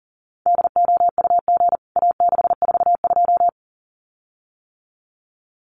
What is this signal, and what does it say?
Morse code, heard as a keyed tone, about 40 wpm, 700 Hz. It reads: DOUG A642
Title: CWT Exchange 40wpm